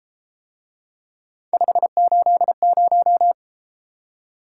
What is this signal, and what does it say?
Morse code, heard as a keyed tone, about 31 wpm, 700 Hz. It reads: 580